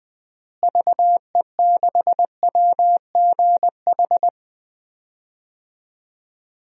Morse code, heard as a keyed tone, about 20 wpm, 700 Hz. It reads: VE6WGH
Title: DXCC-100 Call Signs - 6 Characters - Repeated 2x - 20wpm